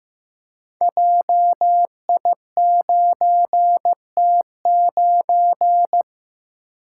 Morse code, heard as a keyed tone, about 15 wpm, 700 Hz. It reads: JI9T9
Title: Code Groups - 5 Letter:Numbers - 15wpm